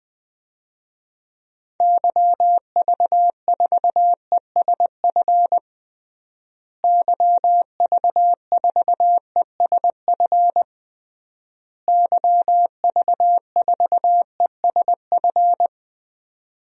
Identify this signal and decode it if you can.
Morse code, heard as a keyed tone, about 20 wpm, 700 Hz. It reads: YV4ESF YV4ESF YV4ESF
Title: DXCC-100 Call Signs - 6 Characters - Repeated 3x - 20wpm